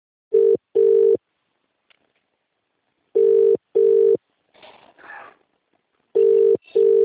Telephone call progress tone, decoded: ringback tone